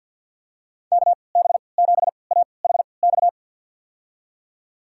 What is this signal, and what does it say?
Morse code, heard as a keyed tone, about 46 wpm, spaced as for 27 wpm, 700 Hz. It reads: KB7AHX
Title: USA Call Signs - 6 Characters - 3x Character Spacing - 50wpm